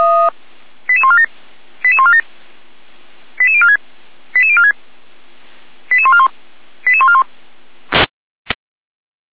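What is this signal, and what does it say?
Five-tone selective call (ZVEI), heard as quick runs of five tones, tones of about 70 ms. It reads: 88147 88147 88046 88046 88141 88141
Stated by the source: ZVEI five-tone selective call received on SDR.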